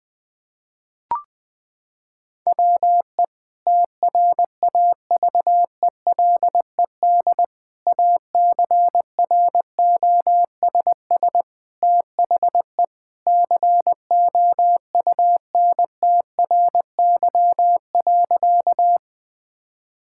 Morse code, heard as a keyed tone, about 20 wpm, 700 Hz. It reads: WE TRAVELED ACROSS THE COUNTRY.
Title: Sentences with 2-Word Phrases - 20wpm